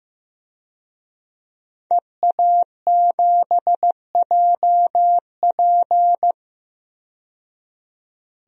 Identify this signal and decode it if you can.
Morse code, heard as a keyed tone, about 15 wpm, 700 Hz. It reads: EA7JP
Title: DXCC-100 Call Signs - 5 Characters - 15wpm